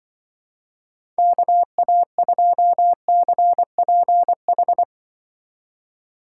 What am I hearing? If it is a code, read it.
Morse code, heard as a keyed tone, about 23 wpm, 700 Hz. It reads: KA2CPH
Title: Speed Racing - USA 2x3 Call Signs 20wpm